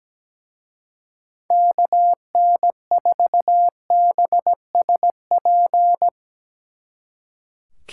Morse code, heard as a keyed tone, about 18 wpm, 700 Hz. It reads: KN4BSP
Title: Speed Racing - USA 2x3 Call Signs 17wpm